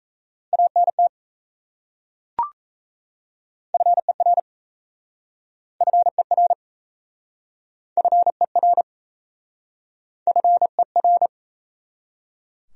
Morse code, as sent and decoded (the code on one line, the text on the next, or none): .- -. - / ..-. .-. / ..-. .-. / ..-. .-. / ..-. .-.
ANT FER FER FER FER